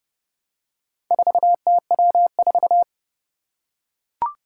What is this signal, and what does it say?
Morse code, heard as a keyed tone, about 32 wpm, 700 Hz. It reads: 4TW4
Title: Four Challenging Letter-Number Combinations 30wpm